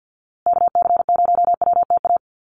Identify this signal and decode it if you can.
Morse code, heard as a keyed tone, about 50 wpm, 700 Hz. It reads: KC0WTA
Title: USA Call Signs Repeated 3x 50wpm